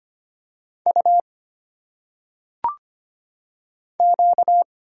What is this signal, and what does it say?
Morse code, heard as a keyed tone, about 25 wpm, 700 Hz. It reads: U Q
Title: Single Letters - Phonetic Pronunciation 25wpm